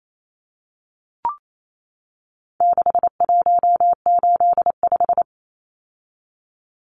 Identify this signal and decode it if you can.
Morse code, heard as a keyed tone, about 28 wpm, 700 Hz. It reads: 6185
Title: Numbers - 4-Digit 28wpm